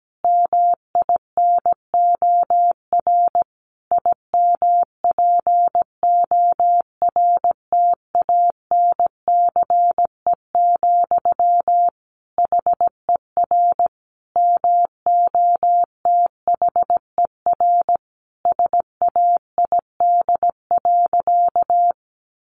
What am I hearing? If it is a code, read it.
Morse code, heard as a keyed tone, about 17 wpm, 700 Hz. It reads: MINOR IMPORTANCE, HER MOTHER SAID.
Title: Sentences from Top 1500 Words 17wpm